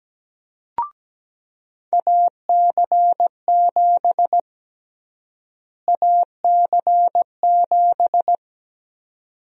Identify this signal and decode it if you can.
Morse code, heard as a keyed tone, about 17 wpm, 700 Hz. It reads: AC7 AC7
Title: USA Call Sign Prefixes - Repeated 2x - 17wpm